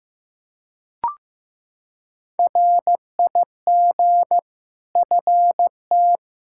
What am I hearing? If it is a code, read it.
Morse code, heard as a keyed tone, about 15 wpm, 700 Hz. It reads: RIG FT
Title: Practice QSO - Exchange 2 - Info - 15wpm